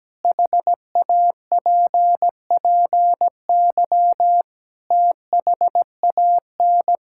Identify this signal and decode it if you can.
Morse code, heard as a keyed tone, about 17 wpm, 700 Hz. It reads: HAPPY THAN
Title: Sentences from Top 1000 Words - Review Part 5 - 17wpm